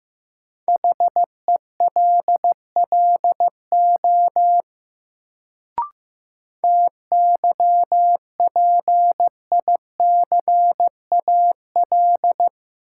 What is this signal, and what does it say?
Morse code, heard as a keyed tone, about 15 wpm, 700 Hz. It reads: HELLO TYPICAL
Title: Sentences from Top 1100 Words - New Word Drill - 15wpm